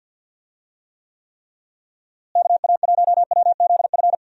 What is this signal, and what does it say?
Morse code, heard as a keyed tone, about 51 wpm, 700 Hz. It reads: KA1WZF
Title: USA Call Signs - 6 Characters - Repeated 3x - 50wpm